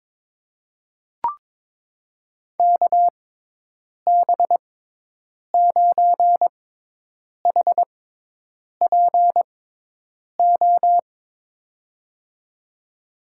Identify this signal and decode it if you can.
Morse code, heard as a keyed tone, about 22 wpm, spaced as for 8 wpm, 700 Hz. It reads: KB9HPO
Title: USA Call Signs - 6 Characters - 6x Character Spacing - 22wpm